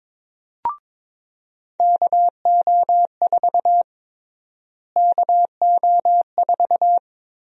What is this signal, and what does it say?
Morse code, heard as a keyed tone, about 22 wpm, 700 Hz. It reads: KO4 KO4
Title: USA Call Sign Prefixes - Repeated 2x - 22wpm